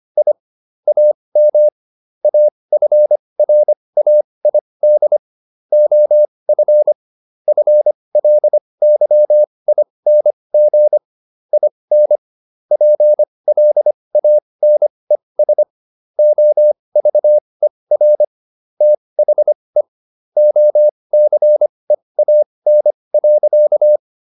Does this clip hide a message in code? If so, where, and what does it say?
Yes: Morse code from 0.2 to 24.0 s, heard as a keyed tone, about 25 wpm, spaced as for 20 wpm, 600 Hz. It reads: I AM AFRAID OF FLYING IN PLANES OVER THE OCEAN.